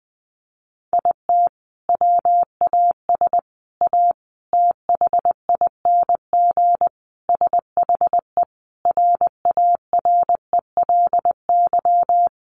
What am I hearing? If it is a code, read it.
Morse code, heard as a keyed tone, about 20 wpm, 700 Hz. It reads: IT WAS A THING SHE RARELY
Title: Sentences from Top 1400 Words 20wpm